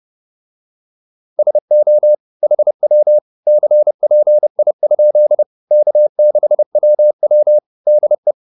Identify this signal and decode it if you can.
Morse code, heard as a keyed tone, about 30 wpm, 600 Hz. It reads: SO HW CPI? K6WW DE